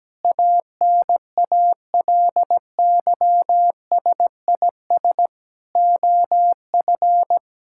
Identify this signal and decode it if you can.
Morse code, heard as a keyed tone, about 17 wpm, 700 Hz. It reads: ANALYSIS OF
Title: Sentences from Top 1000 Words - Review Part 5 - 17wpm